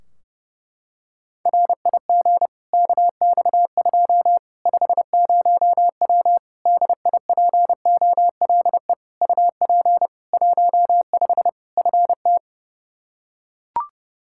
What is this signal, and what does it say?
Morse code, heard as a keyed tone, about 31 wpm, 700 Hz. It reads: RIG KX2 50W DIPOLE UP 15 FT
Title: K1USN SST QSO Practice 30wpm